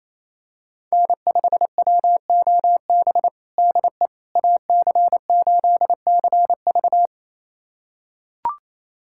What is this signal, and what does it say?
Morse code, heard as a keyed tone, about 28 wpm, 700 Hz. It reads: N5WOB DE AC8CV
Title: USA Callsign DE Callsign - 5 Characters - 28wpm